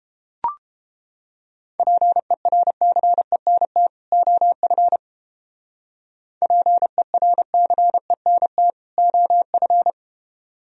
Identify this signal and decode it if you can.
Morse code, heard as a keyed tone, about 30 wpm, 700 Hz. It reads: PERCENT OF PERCENT OF